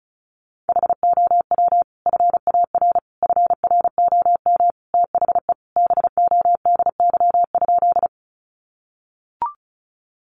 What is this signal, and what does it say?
Morse code, heard as a keyed tone, about 34 wpm, 700 Hz. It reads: HOW FAR FROM THE BODY?
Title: Sentences from Top 400 Words 35wpm